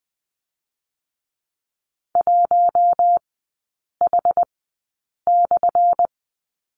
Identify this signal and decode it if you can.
Morse code, heard as a keyed tone, about 20 wpm, 700 Hz. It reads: 1 H /
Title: 198 Lesson 32e - Letter k - Three Characters 20wpm